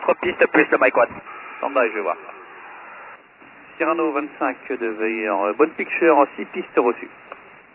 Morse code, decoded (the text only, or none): none